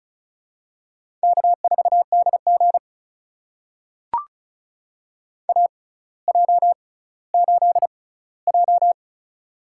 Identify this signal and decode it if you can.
Morse code, heard as a keyed tone, about 36 wpm, spaced as for 18 wpm, 700 Hz. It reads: K4DG AJ8J
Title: USA Call Signs - 4 Characters - 6x Character Spacing - 35wpm